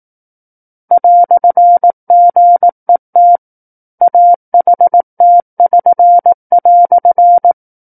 Morse code, heard as a keyed tone, about 18 wpm, 700 Hz. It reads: "GET AHT<SN>"
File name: 18 WPM